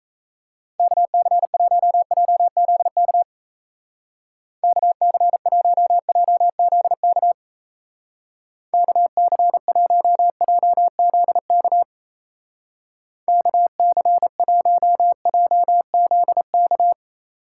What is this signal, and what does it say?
Morse code, heard as a keyed tone, about 35 wpm, 700 Hz. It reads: KC1JZK KC1JZK KC1JZK KC1JZK
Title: Speed Racing - USA Call Signs 28wpm